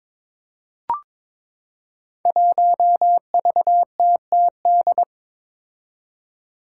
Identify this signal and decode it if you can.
Morse code, heard as a keyed tone, about 22 wpm, 700 Hz. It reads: 1VTTD
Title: Five Challenging Letter-Number Combinations 22wpm